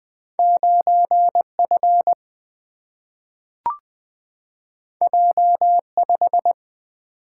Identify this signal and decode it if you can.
Morse code, heard as a keyed tone, about 20 wpm, 700 Hz. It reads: 9F J5
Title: Single-Letter Single-Number 20wpm